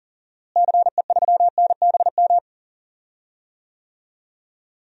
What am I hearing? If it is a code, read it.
Morse code, heard as a keyed tone, about 40 wpm, 700 Hz. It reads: CE3NBM
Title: DXCC-100 Call Signs - 6 Characters - Repeated 3x - 40wpm